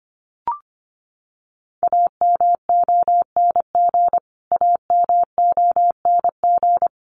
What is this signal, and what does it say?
Morse code, heard as a keyed tone, about 25 wpm, 700 Hz. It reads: AMONG AMONG
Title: Top 500 Words Repeated 25wpm